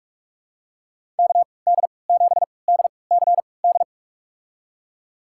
Morse code, heard as a keyed tone, about 47 wpm, spaced as for 26 wpm, 700 Hz. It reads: KD7DCD
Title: USA Call Signs - 3x Character Spacing - 45wpm